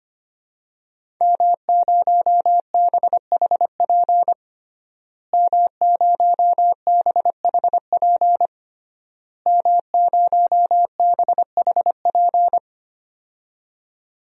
Morse code, heard as a keyed tone, about 25 wpm, 700 Hz. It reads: M0BHP M0BHP M0BHP
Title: DXCC-100 Call Signs - 5 Characters - Repeated 3x - 25wpm